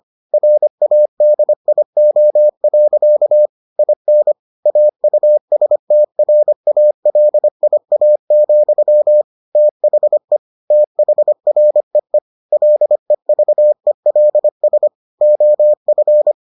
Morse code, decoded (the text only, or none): RADIO. IN AUSTRALIA, THE THREE LEVELS OF